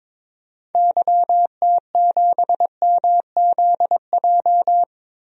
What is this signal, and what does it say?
Morse code, heard as a keyed tone, about 22 wpm, 700 Hz. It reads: YT7MZJ
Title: DXCC-100 Call Signs - 6 Characters - 22wpm